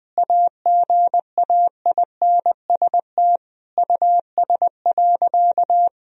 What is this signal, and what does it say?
Morse code, heard as a keyed tone, about 21 wpm, 700 Hz. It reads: AGAINST US.
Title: Sentences from Top 1000 Words - Review Part 2 - 20wpm